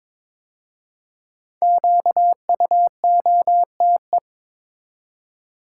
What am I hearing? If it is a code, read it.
Morse code, heard as a keyed tone, about 22 wpm, 700 Hz. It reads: QUOTE
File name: 5 Letter Words - First 3 Letters Given - 22wpm